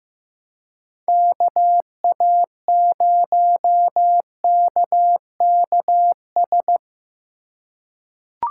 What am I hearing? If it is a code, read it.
Morse code, heard as a keyed tone, about 15 wpm, 700 Hz. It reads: KA0KKS
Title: USA Call Signs - 6 Characters - 2x Character Spacing - 15wpm